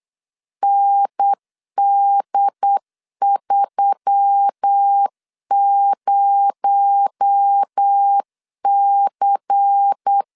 Morse code, heard as a keyed tone, about 8 wpm, 800 Hz. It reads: ND30C
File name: LEZIONI / 8 YJ45 / 8 Y 8wpm 3245